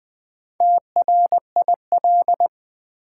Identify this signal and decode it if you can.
Morse code, heard as a keyed tone, about 20 wpm, 700 Hz. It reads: TRIL
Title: Build A Word - 2 Letters At A Time - 8-Letter Words - 20wpm